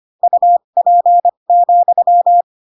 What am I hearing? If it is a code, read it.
Morse code, heard as a keyed tone, about 25 wpm, spaced as for 22 wpm, 700 Hz. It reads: UP,
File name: Pinocchio / Pinocchio ch3 18 25 ik1hge